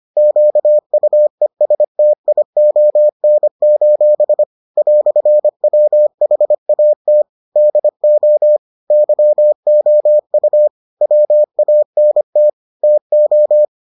Morse code, heard as a keyed tone, about 25 wpm, 600 Hz. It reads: QUESTION: "WHAT DO YOU WANT TO